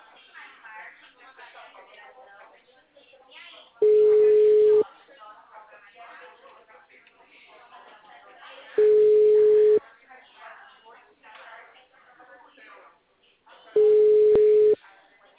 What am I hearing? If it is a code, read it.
Ringback tone (Europe, 425 Hz), with speech.